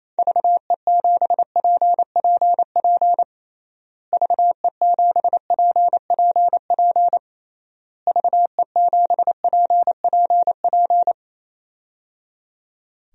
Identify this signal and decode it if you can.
Morse code, heard as a keyed tone, about 27 wpm, 700 Hz. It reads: VE7PPP VE7PPP VE7PPP
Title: DXCC-100 Call Signs - 6 Characters - Repeated 3x - 28wpm